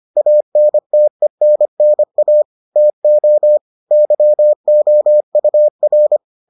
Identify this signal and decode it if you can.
Morse code, heard as a keyed tone, about 25 wpm, 600 Hz. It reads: ANTENNA TO YOUR